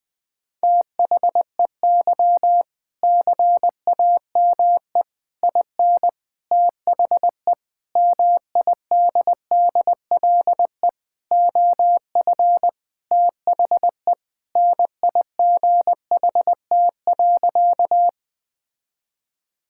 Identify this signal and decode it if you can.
Morse code, heard as a keyed tone, about 21 wpm, 700 Hz. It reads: THEY CAME IN THE MIDDLE OF THE NIGHT.
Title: Sentences from Top 1000 Words - Review Part 6 - 20wpm